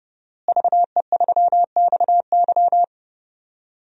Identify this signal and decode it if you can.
Morse code, heard as a keyed tone, about 31 wpm, 700 Hz. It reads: VE3XY
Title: DXCC-100 Call Signs - 5 Characters - Repeated 3x - 30wpm